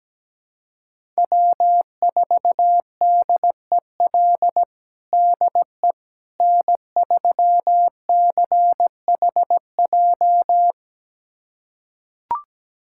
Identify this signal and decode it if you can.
Morse code, heard as a keyed tone, about 17 wpm, 700 Hz. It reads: W4DEL DE N3CHJ
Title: USA Callsign DE Callsign - 5 Characters - 17wpm